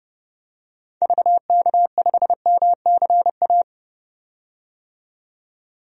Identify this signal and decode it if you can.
Morse code, heard as a keyed tone, about 32 wpm, 700 Hz. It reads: VK5MCA
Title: DXCC-100 Call Signs - 6 Characters - 30wpm